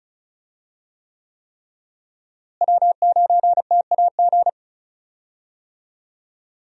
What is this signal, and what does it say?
Morse code, heard as a keyed tone, about 34 wpm, 700 Hz. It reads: W9TAG
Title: USA Call Signs - 5 Characters - Repeated 3x - 35wpm